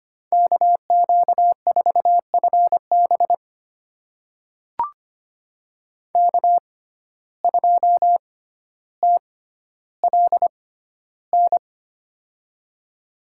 Morse code, heard as a keyed tone, about 25 wpm, spaced as for 13 wpm, 700 Hz. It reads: KQ4FB K2TLN
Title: USA Call Signs - 5 Characters - 6x Character Spacing - 25wpm